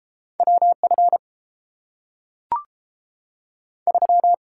Morse code, heard as a keyed tone, about 30 wpm, 700 Hz. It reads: W F 3